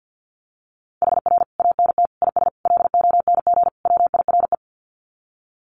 Morse code, heard as a keyed tone, about 50 wpm, 700 Hz. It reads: HR ANT IS LONG WIRE
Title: Common QSO Phrases 50wpm